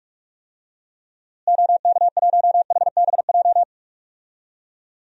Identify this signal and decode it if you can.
Morse code, heard as a keyed tone, about 46 wpm, 700 Hz. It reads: OK1HBJ
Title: DXCC-100 Call Signs - 6 Characters - Repeated 3x - 45wpm